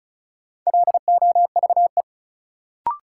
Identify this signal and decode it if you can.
Morse code, heard as a keyed tone, about 34 wpm, 700 Hz. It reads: LOVE
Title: Sentences from Top 400 Words - New Word Drill - 35wpm